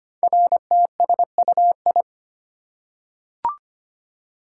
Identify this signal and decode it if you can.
Morse code, heard as a keyed tone, about 25 wpm, 700 Hz. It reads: RTSUI
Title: Code Groups - 5 Letters - 25wpm